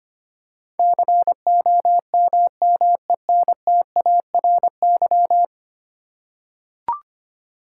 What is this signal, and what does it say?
Morse code, heard as a keyed tone, about 25 wpm, 700 Hz. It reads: COMMENTARY
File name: Ten Letter Words - 25wpm